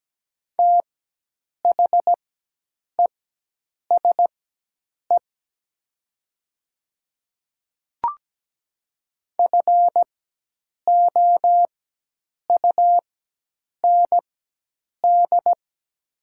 Morse code, heard as a keyed tone, about 17 wpm, spaced as for 8 wpm, 690 Hz. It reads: THESE FOUND
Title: 4x Character Spacing - Top 500 Words - 17wpm